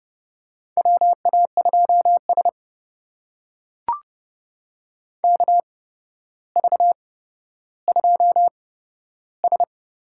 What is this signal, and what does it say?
Morse code, heard as a keyed tone, about 29 wpm, spaced as for 13 wpm, 700 Hz. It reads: WA2S KV2S
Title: USA Call Signs - 4 Characters - 8x Character Spacing - 30wpm